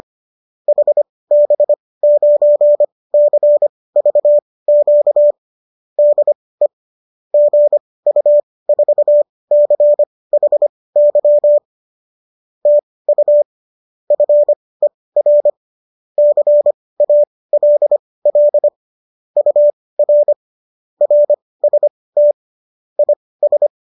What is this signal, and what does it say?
Morse code, heard as a keyed tone, about 25 wpm, spaced as for 18 wpm, 590 Hz. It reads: HB9CVQ DE GU4CHY TU FER CALL UR RST IS